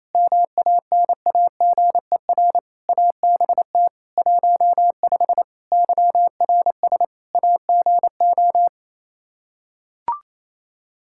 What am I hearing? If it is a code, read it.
Morse code, heard as a keyed tone, about 28 wpm, 700 Hz. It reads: MANAGER ABT 15 YRS AGO
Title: Practice QSO - Exchange 3 - More Info - 28wpm